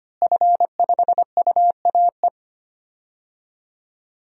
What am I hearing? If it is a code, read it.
Morse code, heard as a keyed tone, about 25 wpm, 700 Hz. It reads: F5UAE
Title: DXCC-100 Call Signs - 5 Characters - Repeated 3x - 25wpm